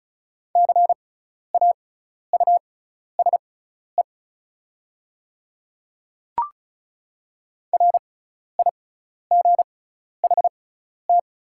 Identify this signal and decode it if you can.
Morse code, heard as a keyed tone, about 34 wpm, spaced as for 12 wpm, 690 Hz. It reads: CAUSE RIGHT
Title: 6x Character Spacing - Top 200 Words - 35wpm